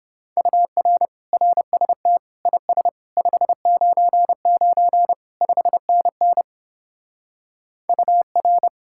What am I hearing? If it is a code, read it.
Morse code, heard as a keyed tone, about 29 wpm, 700 Hz. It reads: UR RST IS 599 5NN UR